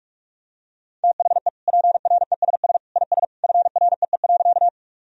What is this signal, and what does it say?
Morse code, heard as a keyed tone, about 43 wpm, 700 Hz. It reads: THE PRESS IS FREE.
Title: Sentences from Top 1000 Words - Review Part 1 - 45wpm